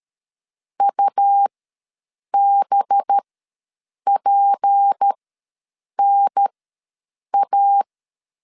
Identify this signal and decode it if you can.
Morse code, heard as a keyed tone, about 13 wpm, spaced as for 7 wpm, 800 Hz. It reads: UBPNA
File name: LEZIONI / 8 YJ45 / 8 YJ4 ws 12wpm 3249